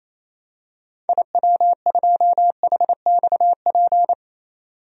Morse code, heard as a keyed tone, about 28 wpm, 700 Hz. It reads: IW2HXP
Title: DXCC-100 Call Signs - 6 Characters - 28wpm